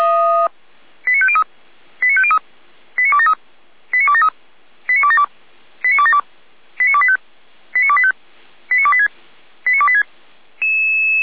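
Five-tone selective call (ZVEI), heard as quick runs of five tones, tones of about 70 ms. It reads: 89503 89503 89273 89273 89272 89272 89285 89285 89286 89286
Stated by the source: ZVEI five-tone selective call received by SDR, archived by Feuerwehr Orlamünde fire brigade.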